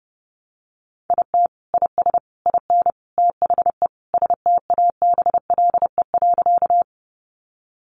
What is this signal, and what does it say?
Morse code, heard as a keyed tone, about 28 wpm, 700 Hz. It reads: IT IS IN THE STABLE.
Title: Sentences from Top 1500 Words 30wpm